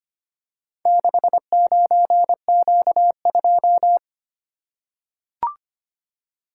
Morse code, heard as a keyed tone, about 25 wpm, 700 Hz. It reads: 69Q2
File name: practice / 25wpm / Four Letter-Number Combinations - 25wpm